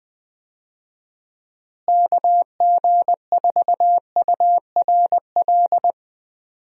Morse code, heard as a keyed tone, about 20 wpm, 700 Hz. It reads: KG4URL
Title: USA Call Signs - 6 Characters - Repeated 2x - 20wpm